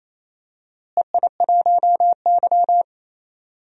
Morse code, heard as a keyed tone, about 27 wpm, 700 Hz. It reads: EI1Y
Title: DXCC-100 Call Signs - 4 Characters - Repeated 2x - 28wpm